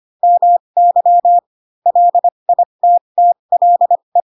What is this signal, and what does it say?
Morse code, heard as a keyed tone, about 25 wpm, spaced as for 22 wpm, 700 Hz. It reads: MY LITTLE